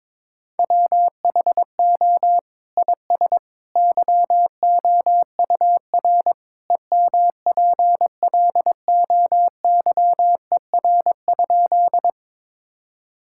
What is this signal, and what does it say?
Morse code, heard as a keyed tone, about 22 wpm, 700 Hz. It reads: WHO IS YOUR EMPLOYER?
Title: Sentences from Top 1600 Words 22wpm